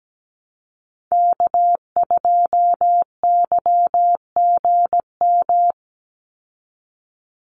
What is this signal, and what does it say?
Morse code, heard as a keyed tone, about 17 wpm, 700 Hz. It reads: K2YGM
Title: Call Signs - 17wpm